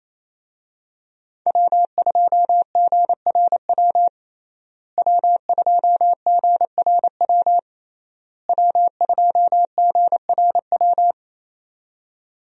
Morse code, heard as a keyed tone, about 28 wpm, 700 Hz. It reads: W2GRW W2GRW W2GRW
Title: USA Call Signs - 5 Characters - Repeated 3x - 28wpm